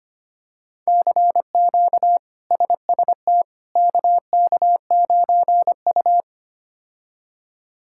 Morse code, heard as a keyed tone, about 25 wpm, 700 Hz. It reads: CQ SST KK9U
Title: K1USN SST QSO Practice - 25wpm